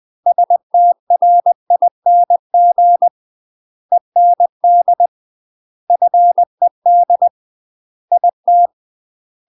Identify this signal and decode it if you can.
Morse code, heard as a keyed tone, about 20 wpm, 700 Hz. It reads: STRING END FED IT
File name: UK MASTER QSOs TRAINING/20WPM